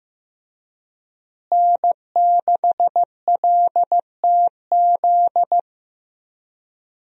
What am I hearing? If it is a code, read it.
Morse code, heard as a keyed tone, about 15 wpm, 700 Hz. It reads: N6LTZ
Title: USA Call Signs - 5 Characters - Repeated 3x - 15wpm